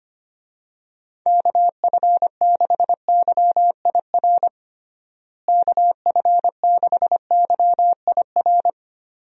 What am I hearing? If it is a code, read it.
Morse code, heard as a keyed tone, about 25 wpm, 700 Hz. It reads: KF6YIR KF6YIR
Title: USA Call Signs - 6 Characters - Repeated 2x - 25wpm